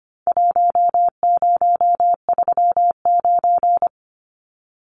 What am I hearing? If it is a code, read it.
Morse code, heard as a keyed tone, about 25 wpm, 700 Hz. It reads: 1039
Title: Numbers - Contest Style - 4-Digit 25wpm